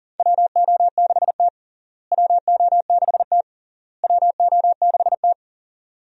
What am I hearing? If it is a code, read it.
Morse code, heard as a keyed tone, about 37 wpm, 700 Hz. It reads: WO6T WO6T WO6T
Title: USA Call Signs - 4 Characters - Repeated 3x - 40wpm